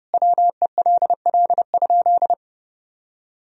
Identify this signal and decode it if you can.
Morse code, heard as a keyed tone, about 31 wpm, 700 Hz. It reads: WELL?
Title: Sentences from Top 400 Words and Abbreviations 30wpm